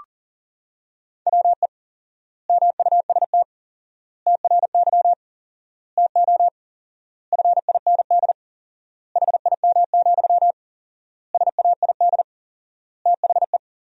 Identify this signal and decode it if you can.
Morse code, heard as a keyed tone, about 40 wpm, 700 Hz. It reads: WE MUST TRY TO FIND HIM, SAID THE
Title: 4x Word-Spacing - Sentences from Top 400 Words - 40wpm